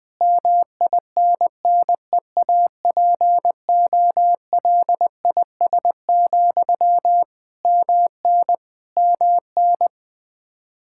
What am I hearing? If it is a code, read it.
Morse code, heard as a keyed tone, about 20 wpm, 700 Hz. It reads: MINNEAPOLIS, MN MN
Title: Practice QSO - Exchange 1 - Beginning - 20wpm